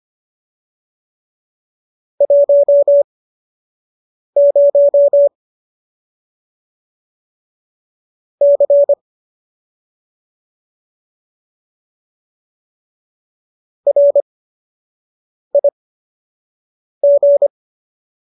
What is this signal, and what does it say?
Morse code, heard as a keyed tone, about 25 wpm, spaced as for 6 wpm, 570 Hz. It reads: 10 C RIG